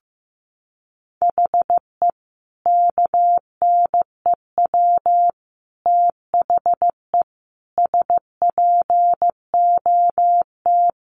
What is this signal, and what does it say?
Morse code, heard as a keyed tone, about 15 wpm, 700 Hz. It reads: HE KNEW THE SPOT
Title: Sentences from Top 1100 Words 15wpm